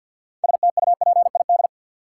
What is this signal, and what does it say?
Morse code, heard as a keyed tone, about 47 wpm, 700 Hz. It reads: STUPID
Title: Six Letter Words - 50wpm